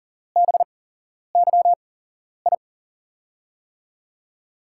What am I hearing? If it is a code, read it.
Morse code, heard as a keyed tone, about 42 wpm, spaced as for 11 wpm, 700 Hz. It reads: BYI
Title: USA Call Signs - 5 Characters - 8x Character Spacing - 40wpm